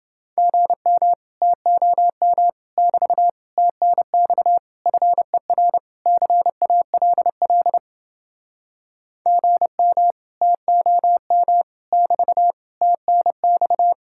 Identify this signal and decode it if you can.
Morse code, heard as a keyed tone, about 30 wpm, 700 Hz. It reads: GM TOM <BT> TNX FER CALL GM TOM <BT> TNX